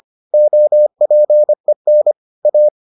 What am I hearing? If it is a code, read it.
Morse code, heard as a keyed tone, about 25 wpm, 600 Hz. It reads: OPEN A